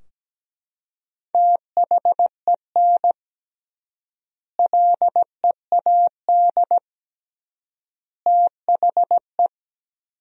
Morse code, heard as a keyed tone, about 17 wpm, 700 Hz. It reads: THEN LEAD THE
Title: 3x Word-Spacing - Sentences from Top 500 Words - 17wpm